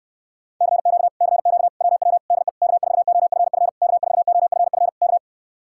Morse code, heard as a keyed tone, about 69 wpm, 700 Hz. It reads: CQ CQ WW DE K4OFF K4OFF K